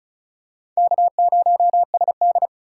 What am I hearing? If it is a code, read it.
Morse code, heard as a keyed tone, about 36 wpm, 690 Hz. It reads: K0SD
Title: USA Call Signs - 4 Characters - 3x Character Spacing - 35wpm